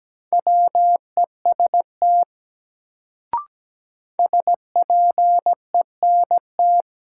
Morse code, heard as a keyed tone, about 17 wpm, 700 Hz. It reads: WEST SPENT
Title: Sets of 5 Words - New Word Drill - 17wpm